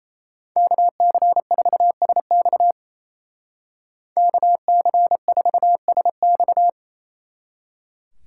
Morse code, heard as a keyed tone, about 34 wpm, 700 Hz. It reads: KC4SX KC4SX
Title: Speed Racing - USA Call Signs 28wpm